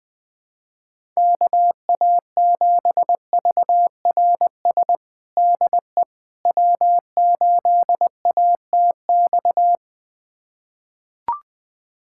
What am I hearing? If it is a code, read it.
Morse code, heard as a keyed tone, about 20 wpm, 700 Hz. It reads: KA7VRS DE W8ATX
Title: USA Callsign DE Callsign 20wpm